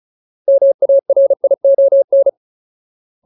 Morse code, heard as a keyed tone, about 34 wpm, 550 Hz. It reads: MARION
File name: Ultimate Word List / 35WPM Ultimate